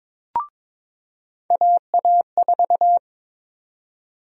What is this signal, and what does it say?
Morse code, heard as a keyed tone, about 22 wpm, 700 Hz. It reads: AA4
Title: USA Call Sign Prefixes - 22wpm